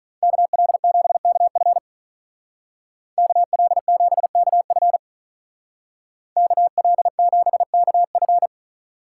Morse code, heard as a keyed tone, about 42 wpm, 700 Hz. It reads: KL7KF KL7KF KL7KF
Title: Speed Racing - USA 2x2 Call Signs 35wpm